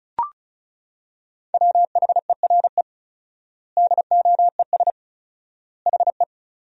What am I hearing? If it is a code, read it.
Morse code, heard as a keyed tone, about 36 wpm, 700 Hz. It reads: WHERE DOES HE
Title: 4x Word-Spacing - Sentences from Top 200 Words - 35wpm